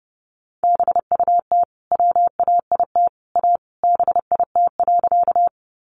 Morse code, heard as a keyed tone, about 30 wpm, 700 Hz. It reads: BUT WAIT A BIT.
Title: Sentences from Top 900 Words 30wpm